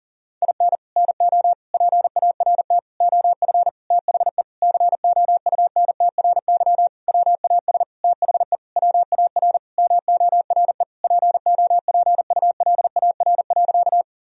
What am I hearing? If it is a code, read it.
Morse code, heard as a keyed tone, about 37 wpm, 700 Hz. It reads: IN NO PART OF THE COUNTRY WAS THE WAR MORE POPULAR.
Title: Sentences from Top 1000 Words - Review Part 3 - 40wpm